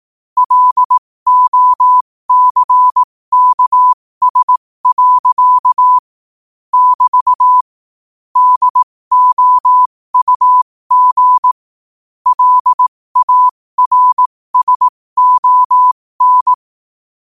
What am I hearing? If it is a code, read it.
Morse code, heard as a keyed tone, about 18 wpm, spaced as for 15 wpm, 1000 Hz. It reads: LOCKS. <BT> DOUG LARSON
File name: qotdfiles / 15wpm